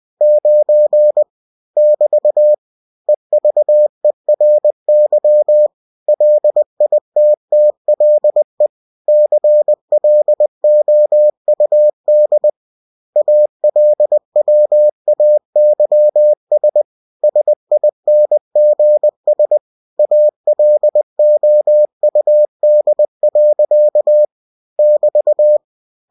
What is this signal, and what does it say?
Morse code, heard as a keyed tone, about 20 wpm, 600 Hz. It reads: 9 <BT> EVERY LITTLE CLOUD ALWAYS SINGS ALOUD. <BT>
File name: cw-books / winnie-the-pooh / winnie-20wpm-page-0009